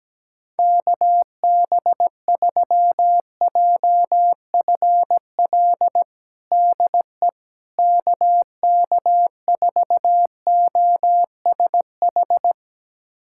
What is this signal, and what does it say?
Morse code, heard as a keyed tone, about 17 wpm, 700 Hz. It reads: KB3JFL DE KK4OSH
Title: USA Callsign DE Callsign 17wpm